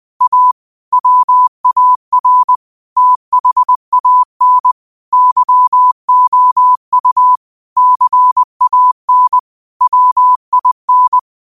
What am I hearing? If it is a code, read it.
Morse code, heard as a keyed tone, about 20 wpm, 1000 Hz. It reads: A WAR THAN YOU CAN WIN
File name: qotdfiles / 20wpm